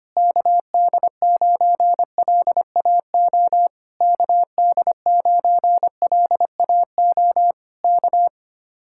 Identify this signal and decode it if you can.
Morse code, heard as a keyed tone, about 24 wpm, 700 Hz. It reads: KD9LAO KD9LAO K
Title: Speed-Racing Common QSO Phrases - 25wpm